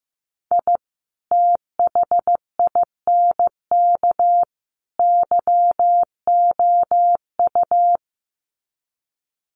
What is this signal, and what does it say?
Morse code, heard as a keyed tone, about 15 wpm, 700 Hz. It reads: I THINK YOU
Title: Sets of 3 Words 15wpm